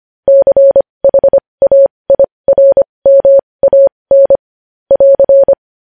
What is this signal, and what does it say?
Morse code, heard as a keyed tone, about 25 wpm, spaced as for 20 wpm, 550 Hz. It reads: CHAIRMAN <AR>
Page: news-headlines-in-morse-code-at-20-wpm